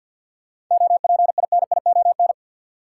Morse code, heard as a keyed tone, about 52 wpm, 700 Hz. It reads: OPINION
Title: Sentences from Top 1200 Words - New Word Drill - 50wpm